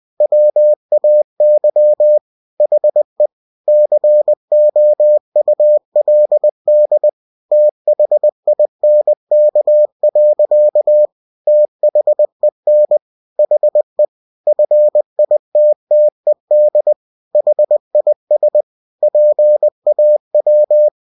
Morse code, heard as a keyed tone, about 20 wpm, 600 Hz. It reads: WAY HE COULD THINK. THEN HE FITTED HIS PAW
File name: cw-books / winnie-the-pooh / winnie-20wpm-page-0025